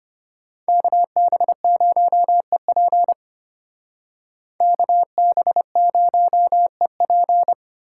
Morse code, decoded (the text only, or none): KB0EP KB0EP